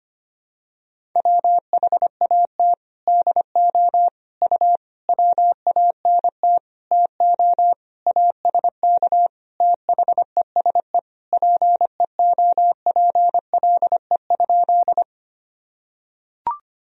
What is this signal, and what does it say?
Morse code, heard as a keyed tone, about 25 wpm, 700 Hz. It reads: WHAT DO U WANT TO ASK THESE PEOPLE?
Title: Sentences from Top 200 Words and Abbreviations 25wpm